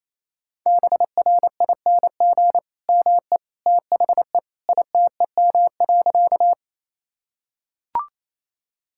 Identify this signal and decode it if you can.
Morse code, heard as a keyed tone, about 27 wpm, 700 Hz. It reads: BRING ME THE ITEM.
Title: Sentences from Top 1000 Words - Review Part 6 - 28wpm